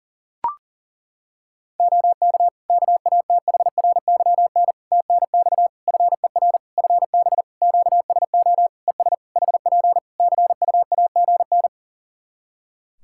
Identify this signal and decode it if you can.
Morse code, heard as a keyed tone, about 40 wpm, 700 Hz. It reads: OK KATHRYN TNX FER FB QSO ES HP CUAGN
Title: Practice QSO - Exchange 4 - Ending - 40wpm